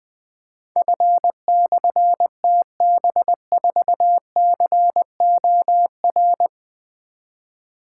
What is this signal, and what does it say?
Morse code, heard as a keyed tone, about 20 wpm, 700 Hz. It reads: F/TB4COR
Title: Difficult Call Signs 20wpm